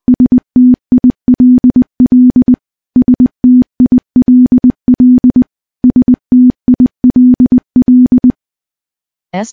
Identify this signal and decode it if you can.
Morse code, heard as a keyed tone, about 20 wpm, 250 Hz. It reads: STILL STILL STILL
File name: morse-training-20WPM